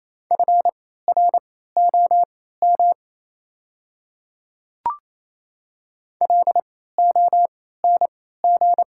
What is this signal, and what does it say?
Morse code, heard as a keyed tone, about 28 wpm, spaced as for 16 wpm, 700 Hz. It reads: FROM LONG
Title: 3x Character Spacing - Top 200 Words - 28wpm